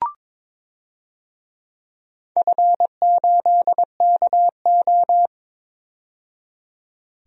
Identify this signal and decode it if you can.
Morse code, heard as a keyed tone, about 22 wpm, 700 Hz. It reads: F8KO
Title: DXCC-100 Call Signs - 4 Characters - 22wpm